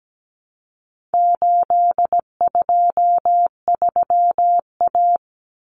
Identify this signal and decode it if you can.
Morse code, heard as a keyed tone, about 17 wpm, 700 Hz. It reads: 823A
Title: Common Cut Numbers - 4-Digit 17wpm